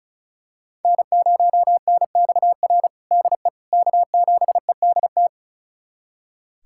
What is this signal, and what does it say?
Morse code, heard as a keyed tone, about 34 wpm, 700 Hz. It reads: N0NXR DE K7EDT
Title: USA Callsign DE Callsign - 5 Characters - 35wpm